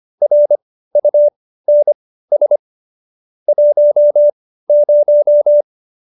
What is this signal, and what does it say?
Morse code, heard as a keyed tone, about 25 wpm, spaced as for 15 wpm, 590 Hz. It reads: RUNS 10